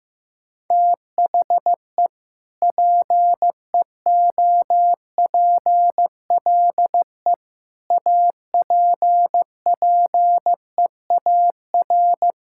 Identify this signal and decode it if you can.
Morse code, heard as a keyed tone, about 15 wpm, 700 Hz. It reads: THE PEOPLE APPEAR
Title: Sentences from Top 1000 Words - Review Part 5 - 15wpm